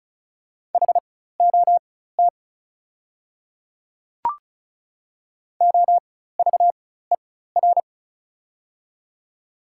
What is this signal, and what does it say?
Morse code, heard as a keyed tone, about 36 wpm, spaced as for 16 wpm, 700 Hz. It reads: HOT OVER
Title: 4x Character Spacing - Top 100 Words - 35wpm